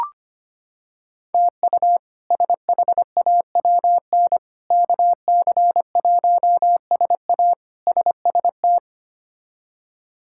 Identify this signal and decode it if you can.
Morse code, heard as a keyed tone, about 25 wpm, 700 Hz. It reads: TU SHAWN KC1SA SST
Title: K1USN SST QSO Practice - 25wpm